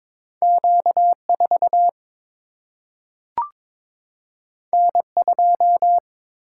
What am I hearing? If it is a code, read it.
Morse code, heard as a keyed tone, about 22 wpm, 700 Hz. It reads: Q4 N2
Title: Single-Letter Single-Number 22wpm